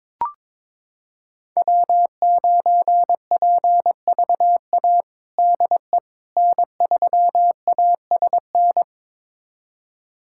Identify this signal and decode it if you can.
Morse code, heard as a keyed tone, about 22 wpm, 700 Hz. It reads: W9PVA DE N3ASN
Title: USA Callsign DE Callsign - 5 Characters - 22wpm